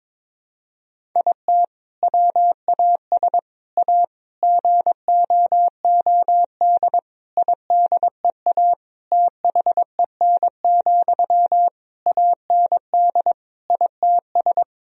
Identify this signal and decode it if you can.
Morse code, heard as a keyed tone, about 22 wpm, 700 Hz. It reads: IT WAS A GOOD IDEA THEN, AND ITS
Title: Sentences from Top 1000 Words - Review Part 4 - 22wpm